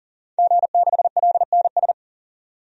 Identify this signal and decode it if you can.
Morse code, heard as a keyed tone, about 40 wpm, 700 Hz. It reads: G6LNS
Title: DXCC-100 Call Signs - 5 Characters - 40wpm